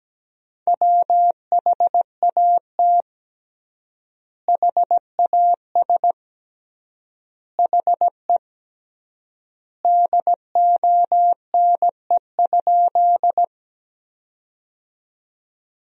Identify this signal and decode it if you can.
Morse code, heard as a keyed tone, about 17 wpm, 700 Hz. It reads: WHAT HAS HE DONE?
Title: 3x Word-Spacing - Sentences from Top 500 Words - 17wpm